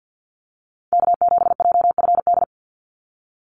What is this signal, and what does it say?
Morse code, heard as a keyed tone, about 50 wpm, 700 Hz. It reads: K7JFD
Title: NA SOTA Chaser Callsigns 50wpm